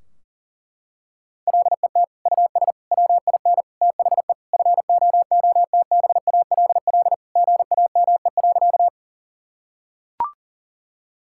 Morse code, heard as a keyed tone, about 37 wpm, 700 Hz. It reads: LET US WIN THE FOOTBALL GAME.
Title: Sentences from Top 1600 Words 40wpm